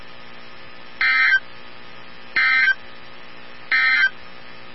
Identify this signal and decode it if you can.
Emergency Alert System (SAME) tones, heard as harsh buzzing bursts. It is heard from 0.9 to 4.2 s.
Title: NOAA Weather Radio 162.550